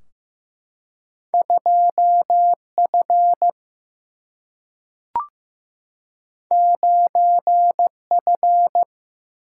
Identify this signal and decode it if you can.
Morse code, heard as a keyed tone, about 15 wpm, 700 Hz. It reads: 2F 9F
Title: Single-Letter Single-Number 15wpm